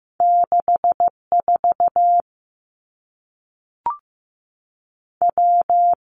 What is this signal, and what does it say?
Morse code, heard as a keyed tone, about 15 wpm, 700 Hz. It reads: 6 4 W
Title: Two Challenging Letter-Number Combinations 15wpm